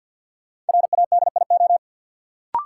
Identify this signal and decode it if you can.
Morse code, heard as a keyed tone, about 53 wpm, 700 Hz. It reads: RADIO
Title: Sentences from Top 1000 Words - Review Part 3 - New Word Drill - 50wpm